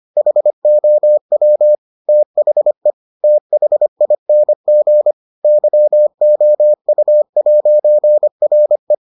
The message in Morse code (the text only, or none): HOW THE THING YOU'RE